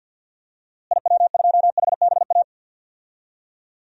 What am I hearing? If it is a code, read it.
Morse code, heard as a keyed tone, about 51 wpm, 700 Hz. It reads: IW2HBA
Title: DXCC-100 Call Signs - 6 Characters - Repeated 3x - 50wpm